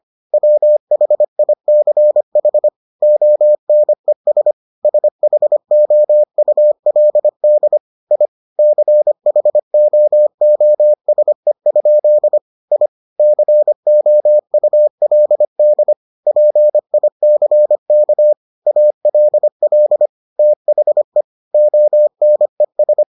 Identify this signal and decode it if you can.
Morse code, heard as a keyed tone, about 25 wpm, 600 Hz. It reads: WHICH ONES SHOULD I CHOOSE? I COULD PICK ALL THE ONES